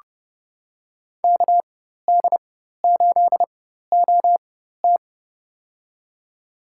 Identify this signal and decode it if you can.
Morse code, heard as a keyed tone, about 30 wpm, spaced as for 14 wpm, 700 Hz. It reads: KD8OT
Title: USA Call Signs - 4x Character Spacing - 30wpm